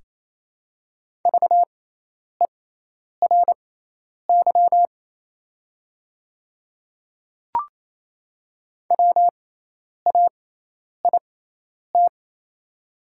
Morse code, heard as a keyed tone, about 28 wpm, spaced as for 10 wpm, 700 Hz. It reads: VERY WAIT
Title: 6x Character Spacing - Top 500 Words - 28wpm